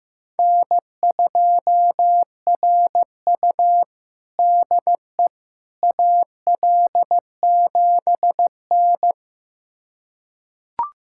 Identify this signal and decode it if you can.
Morse code, heard as a keyed tone, about 15 wpm, 700 Hz. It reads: N2RU DE AL7N
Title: USA Callsign DE Callsign - 4 Characters - 15wpm